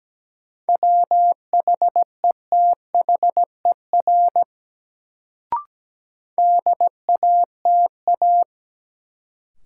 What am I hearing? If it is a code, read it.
Morse code, heard as a keyed tone, about 17 wpm, 700 Hz. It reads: WHETHER DATA
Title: Sentences from Top 1000 Words - Review Part 3 - New Word Drill - 17wpm